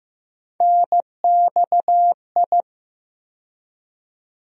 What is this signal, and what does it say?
Morse code, heard as a keyed tone, about 15 wpm, 700 Hz. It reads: NXI
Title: USA Call Sign Postfixes - 3 Letters 15wpm